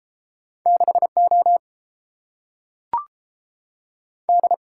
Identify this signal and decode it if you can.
Morse code, heard as a keyed tone, about 31 wpm, 700 Hz. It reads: 6O D